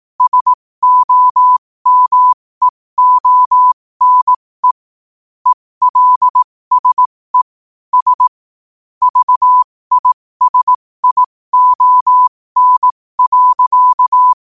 Morse code, heard as a keyed tone, about 18 wpm, spaced as for 15 wpm, 1000 Hz. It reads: SOMEONE ELSE S VISION.
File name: qotdfiles / 15wpm